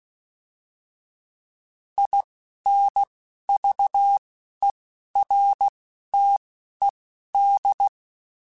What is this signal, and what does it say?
Morse code, heard as a keyed tone, about 16 wpm, spaced as for 12 wpm, 800 Hz. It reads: INVERTED
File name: fromSMs / FromSMs mp3s / Antennas@10